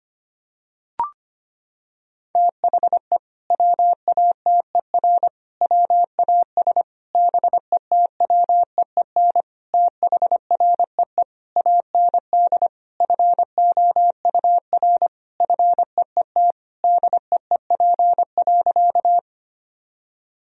Morse code, heard as a keyed tone, about 25 wpm, 700 Hz. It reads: THE WATER WAS BETWEEN THREE AND FOUR FEET DEEP.
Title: Sentences from Top 1000 Words - Review Part 3 - 25wpm